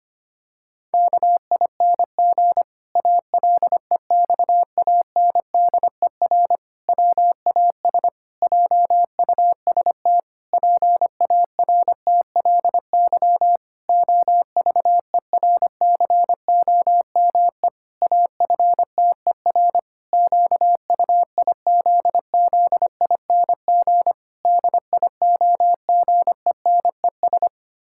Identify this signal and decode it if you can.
Morse code, heard as a keyed tone, about 25 wpm, 700 Hz. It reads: KING ALEXANDER WAS JUST PARTLY OVERCOME AFTER QUIZZING DIOGENES
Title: Pangram Challenge v2 - 25wpm